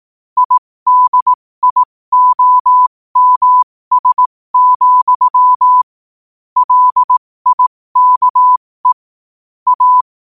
Morse code, heard as a keyed tone, about 18 wpm, spaced as for 15 wpm, 1000 Hz. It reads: IDIOMS, LIKE A
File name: qotdfiles / 15wpm